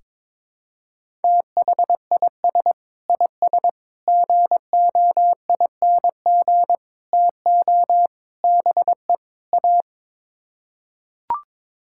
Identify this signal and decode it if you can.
Morse code, heard as a keyed tone, about 22 wpm, 700 Hz. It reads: THIS IS GOING TO BE A
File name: practice / 22wpm / Sets of 6 Words - 22wpm